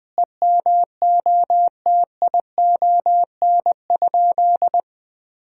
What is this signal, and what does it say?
Morse code, heard as a keyed tone, about 21 wpm, 700 Hz. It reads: EMOTION?
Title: Sentences from Top 1600 Words 20wpm